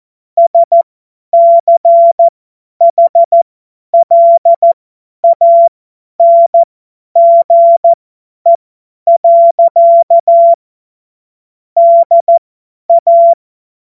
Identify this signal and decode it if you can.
Morse code, heard as a keyed tone, about 14 wpm, spaced as for 10 wpm, 680 Hz. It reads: SCHLANGE. DA
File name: Morsekurs / Witze / Witz 20 Februar 2024 50 F70